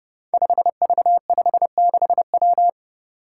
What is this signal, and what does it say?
Morse code, heard as a keyed tone, about 30 wpm, 700 Hz. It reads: 5V56W
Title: Five Challenging Letter-Number Combinations 30wpm